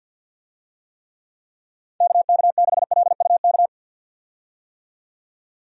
Morse code, heard as a keyed tone, about 51 wpm, 700 Hz. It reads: KK6LUK
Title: USA Call Signs - 6 Characters - 50wpm